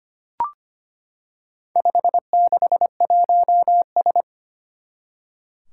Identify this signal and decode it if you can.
Morse code, heard as a keyed tone, about 25 wpm, 700 Hz. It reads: 561S
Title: Four Challenging Letter-Number Combinations 25wpm